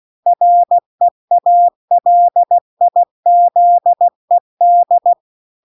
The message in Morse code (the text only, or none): REALIZED